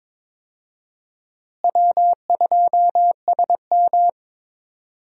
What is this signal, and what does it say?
Morse code, heard as a keyed tone, about 22 wpm, 700 Hz. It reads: W2SM
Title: USA Call Signs - 4 Characters - Repeated 3x - 22wpm